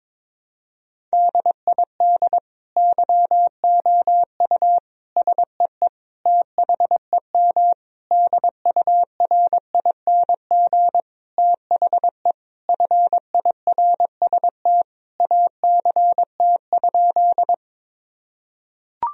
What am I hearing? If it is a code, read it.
Morse code, heard as a keyed tone, about 22 wpm, 700 Hz. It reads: DID YOU SEE THEM DURING THE FIRST ACT?
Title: Sentences from Top 1000 Words - Review Part 2 - 22wpm